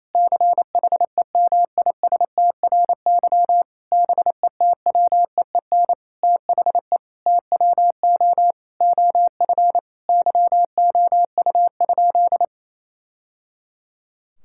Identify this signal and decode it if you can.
Morse code, heard as a keyed tone, about 28 wpm, 700 Hz. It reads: CHEMISTRY BETWEEN THE TWO OF YOU?
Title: Sentences from Top 1000 Words - Review Part 6 - 28wpm